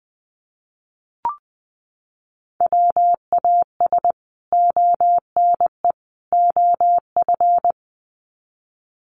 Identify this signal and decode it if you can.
Morse code, heard as a keyed tone, about 20 wpm, 700 Hz. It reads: WAS ONE OF
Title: Sets of 3 Words 20wpm